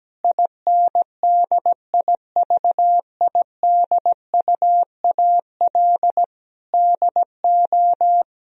Morse code, heard as a keyed tone, about 17 wpm, 700 Hz. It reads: INDIVIDUAL DO
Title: Sentences from Top 1000 Words - Review Part 4 - 17wpm